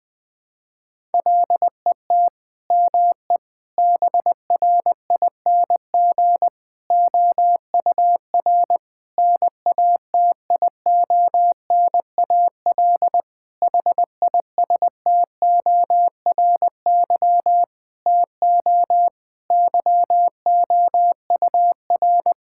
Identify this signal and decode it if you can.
Morse code, heard as a keyed tone, about 20 wpm, 700 Hz. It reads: LET ME BRING OUR NATIONAL HISTORY TO YOUR
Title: Sentences from Top 1000 Words - Review Part 6 - 20wpm